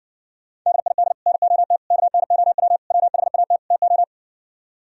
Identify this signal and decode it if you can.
Morse code, heard as a keyed tone, about 61 wpm, 700 Hz. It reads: DID NOT KNOW WHAT TO